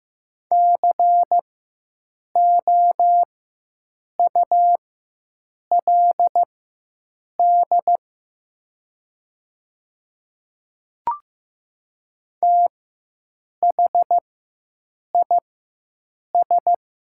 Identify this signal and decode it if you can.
Morse code, heard as a keyed tone, about 15 wpm, spaced as for 7 wpm, 700 Hz. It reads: COULD THIS
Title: 4x Character Spacing - Top 100 Words - 15wpm